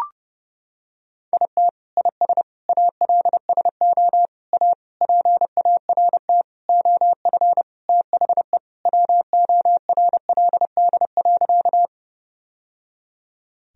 Morse code, heard as a keyed tone, about 29 wpm, 700 Hz. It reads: IT IS ALSO A PART OF THE WORLD.
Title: Sentences from Top 1000 Words - Review Part 1 - 30wpm